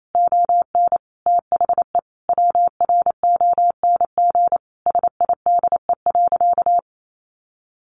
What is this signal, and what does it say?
Morse code, heard as a keyed tone, about 28 wpm, 700 Hz. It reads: ON THE WRONG SIDE.
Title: Sentences from Top 900 Words 28wpm